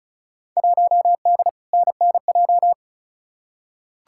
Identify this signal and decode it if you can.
Morse code, heard as a keyed tone, about 36 wpm, 700 Hz. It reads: 1D NNJ
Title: Field Day Exchanges Encore - 35wpm